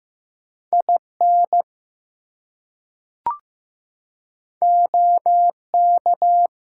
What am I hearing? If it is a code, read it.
Morse code, heard as a keyed tone, about 15 wpm, 700 Hz. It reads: IN OK
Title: Two Letter Words - 15wpm